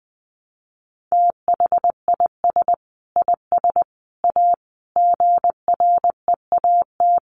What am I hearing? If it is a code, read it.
Morse code, heard as a keyed tone, about 20 wpm, 700 Hz. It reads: THIS IS A GREAT
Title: Sentences from Top 1300 Words 20wpm